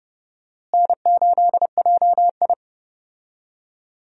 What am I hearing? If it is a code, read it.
Morse code, heard as a keyed tone, about 30 wpm, 700 Hz. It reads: N8JI
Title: USA Call Signs - 4 Characters - Repeated 3x - 30wpm